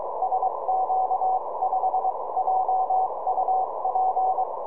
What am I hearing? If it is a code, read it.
Morse code, heard as a keyed tone, about 23 wpm, spaced as for 20 wpm, 750 Hz, with noise. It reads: SQ5JUP
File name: tropo / sq5jup 70cm